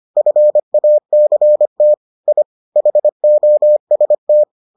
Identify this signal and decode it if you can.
Morse code, heard as a keyed tone, about 25 wpm, 600 Hz. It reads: FACT I HOST